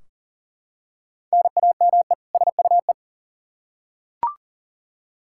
Morse code, heard as a keyed tone, about 37 wpm, 700 Hz. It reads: NAME SUE
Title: Practice QSO - Exchange 1 - Beginning - 40wpm